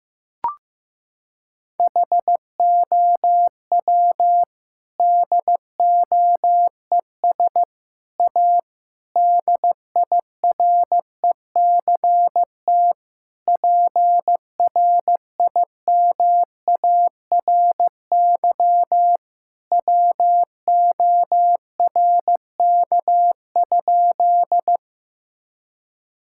Morse code, heard as a keyed tone, about 15 wpm, 700 Hz. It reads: HOW DOES A DIRECT PRIMARY WORK?
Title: Sentences from Top 1000 Words - Review Part 5 - 15wpm